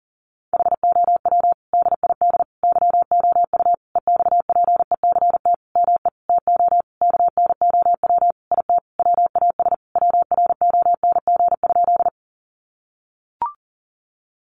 Morse code, heard as a keyed tone, about 37 wpm, 700 Hz. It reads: HOW DID YOU EXPECT ME TO KNOW IT WAS WRONG?
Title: Sentences from Top 1100 Words 40wpm